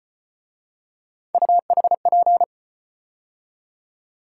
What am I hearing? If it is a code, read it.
Morse code, heard as a keyed tone, about 34 wpm, 700 Hz. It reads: UHP